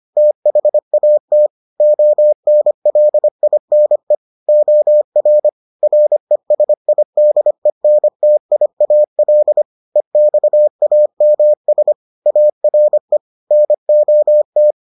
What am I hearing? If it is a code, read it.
Morse code, heard as a keyed tone, about 25 wpm, 600 Hz. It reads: THAT ONLINE OR RESIDENTIAL EXAMS ARE NOT